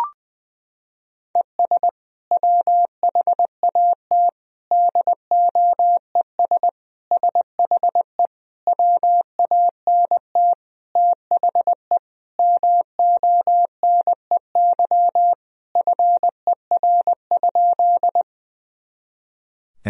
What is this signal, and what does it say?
Morse code, heard as a keyed tone, about 21 wpm, 700 Hz. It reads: ES WHAT DOES SHE WANT THE MONEY FER?
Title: Sentences from Top 400 Words and Abbreviations 20wpm